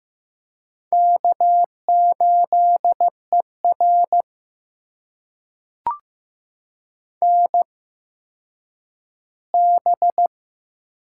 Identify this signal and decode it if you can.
Morse code, heard as a keyed tone, about 15 wpm, spaced as for 9 wpm, 700 Hz. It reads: K8ER NB
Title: USA Call Signs - 4 Characters - 8x Character Spacing - 15wpm